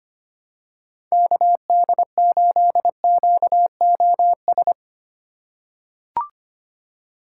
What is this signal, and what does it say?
Morse code, heard as a keyed tone, about 25 wpm, 700 Hz. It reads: KD8QOS
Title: USA Call Signs - 6 Characters - 2x Character Spacing - 25wpm